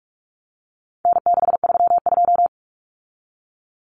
Morse code, heard as a keyed tone, about 44 wpm, 700 Hz. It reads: N632
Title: Common Cut Numbers - 4-Digit 45wpm